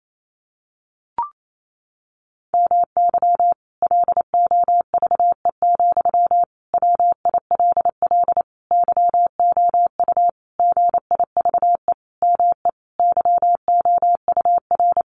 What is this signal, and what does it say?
Morse code, heard as a keyed tone, about 28 wpm, 700 Hz. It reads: MY LOVE, WILL YOU GIVE ME YOUR
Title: Sentences from Top 1200 Words 28wpm